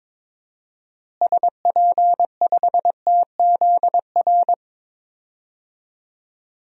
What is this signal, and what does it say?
Morse code, heard as a keyed tone, about 22 wpm, 700 Hz. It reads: SP5TZR
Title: DXCC-100 Call Signs - 6 Characters - Repeated 3x - 22wpm